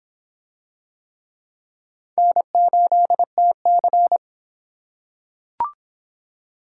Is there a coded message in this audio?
Yes: Morse code from 2.2 to 4.2 s, heard as a keyed tone, about 25 wpm, 700 Hz. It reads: N8TC